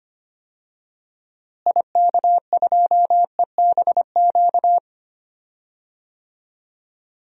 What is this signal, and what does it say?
Morse code, heard as a keyed tone, about 25 wpm, 700 Hz. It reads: IK2EBQ
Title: DXCC-100 Call Signs - 6 Characters - Repeated 3x - 25wpm